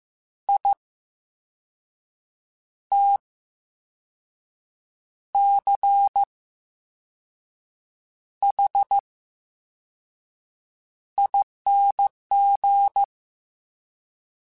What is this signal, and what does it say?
Morse code, heard as a keyed tone, about 15 wpm, 800 Hz. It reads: I T C H I N G
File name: Special INGs-mp3s / Special INGs-mp3s / VerySlowToFaster Ing @13